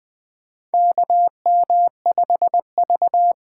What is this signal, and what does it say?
Morse code, heard as a keyed tone, about 20 wpm, 700 Hz. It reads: KM5V
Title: USA Call Signs - 8x Character Spacing - 20wpm